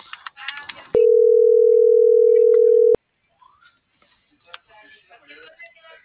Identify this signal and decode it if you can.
Ringback tone (North America), 440 + 480 Hz.